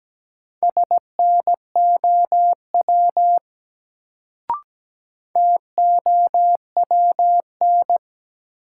Morse code, heard as a keyed tone, about 17 wpm, 700 Hz. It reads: SNOW TOWN
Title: Sentences from Top 500 Words - New Word Drill - 17wpm